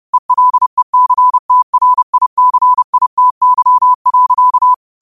Morse code, heard as a keyed tone, about 30 wpm, 1000 Hz. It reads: ELECTRICITY.